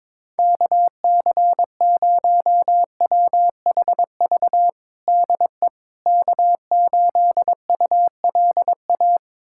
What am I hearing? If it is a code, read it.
Morse code, heard as a keyed tone, about 22 wpm, 700 Hz. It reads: KC0WHV DE K8ULA
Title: USA Callsign DE Callsign 22wpm